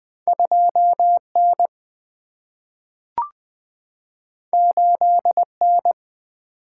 Morse code, heard as a keyed tone, about 20 wpm, 700 Hz. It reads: 2 N 8 N
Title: Single-Letter Single-Number 20wpm